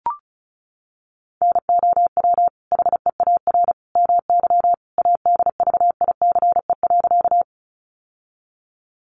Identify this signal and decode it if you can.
Morse code, heard as a keyed tone, about 34 wpm, 700 Hz. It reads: NOW HEAR MY ADVICE.
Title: Sentences from Top 1100 Words 35wpm